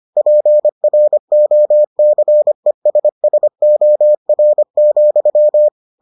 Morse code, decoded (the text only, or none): PROCESSOR,